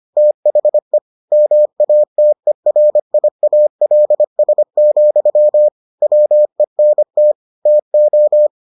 Morse code, heard as a keyed tone, about 25 wpm, 600 Hz. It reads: THE MATERIALS, WENT TO